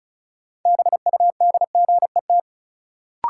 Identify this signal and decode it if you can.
Morse code, heard as a keyed tone, about 34 wpm, 700 Hz. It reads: BUDGET
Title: Sentences from Top 1200 Words - New Word Drill - 35wpm